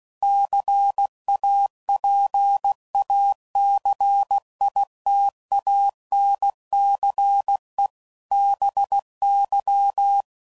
Morse code, heard as a keyed tone, about 16 wpm, 800 Hz. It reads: CAPACITANCE BY
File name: CW Code Groups @15WPM / Technical Subjects / XC